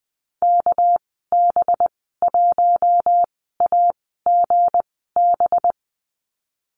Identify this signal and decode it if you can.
Morse code, heard as a keyed tone, about 19 wpm, spaced as for 14 wpm, 700 Hz. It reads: KB1AGB
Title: 213 Lesson 34g - Number 8 - Call Signs 20wpm